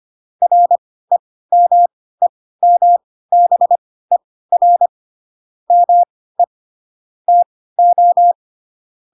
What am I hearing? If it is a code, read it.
Morse code, heard as a keyed tone, about 25 wpm, spaced as for 16 wpm, 700 Hz. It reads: REMEMBER ME TO
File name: Pinocchio / Pinocchio ch7 10 25 ik1hge